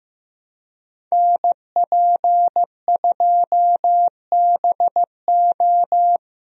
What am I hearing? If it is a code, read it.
Morse code, heard as a keyed tone, about 15 wpm, 700 Hz. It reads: NP2BO
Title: USA Call Signs - 5 Characters - Repeated 2x - 15wpm